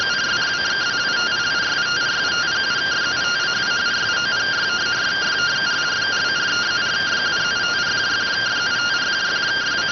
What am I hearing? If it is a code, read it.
Two-tone teleprinter signal, about 170 Hz shift.